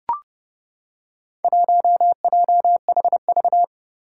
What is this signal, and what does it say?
Morse code, heard as a keyed tone, about 29 wpm, 700 Hz. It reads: 1JHV
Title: Four Challenging Letter-Number Combinations 30wpm